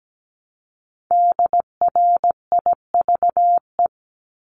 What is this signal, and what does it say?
Morse code, heard as a keyed tone, about 17 wpm, 700 Hz. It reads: DRIVE
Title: Multi-letter Phonemes Repeated 3x 17wpm